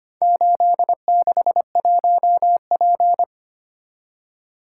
Morse code, heard as a keyed tone, about 26 wpm, 700 Hz. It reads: 861P
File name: practice / 25wpm / Four Letter-Number Combinations - 25wpm